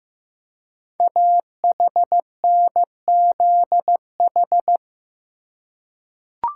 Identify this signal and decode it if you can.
Morse code, heard as a keyed tone, about 15 wpm, 700 Hz. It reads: AHNZH
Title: Code Groups - 5 Letters - 15wpm